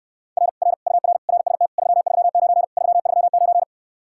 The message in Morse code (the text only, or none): R R UR RST 339 339